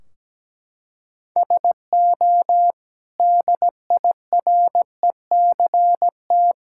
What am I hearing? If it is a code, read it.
Morse code, heard as a keyed tone, about 17 wpm, 700 Hz. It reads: SO DIRECT
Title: Sentences from Top 1600 Words 17wpm